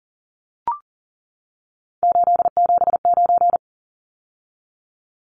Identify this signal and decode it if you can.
Morse code, heard as a keyed tone, about 40 wpm, 700 Hz. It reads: 879
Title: Numbers - 3-Digit 40wpm